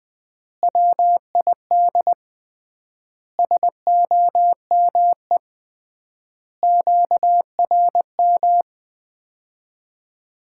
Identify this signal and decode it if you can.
Morse code, heard as a keyed tone, about 20 wpm, 700 Hz. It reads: WID SOME QRM
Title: Common QSO Phrases - 3x Word-Spacing - 20wpm